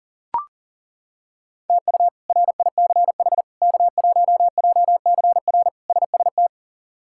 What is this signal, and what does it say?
Morse code, heard as a keyed tone, about 40 wpm, 700 Hz. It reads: TU RICH K1JCR SST
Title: K1USN SST QSO Practice - 40wpm